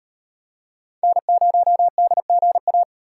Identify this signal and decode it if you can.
Morse code, heard as a keyed tone, about 38 wpm, 700 Hz. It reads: N0DGA